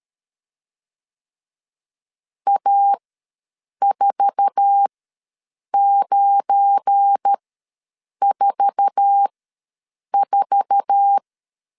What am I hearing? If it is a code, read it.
Morse code, heard as a keyed tone, about 13 wpm, spaced as for 7 wpm, 800 Hz. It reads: A4944
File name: LEZIONI / 8 YJ45 / 8 YJ4 ws 12wpm 3249